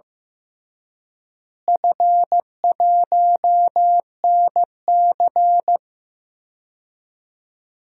Morse code, heard as a keyed tone, about 15 wpm, 700 Hz. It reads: F1NC
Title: DXCC-100 Call Signs - 4 Characters - Repeated 2x - 15wpm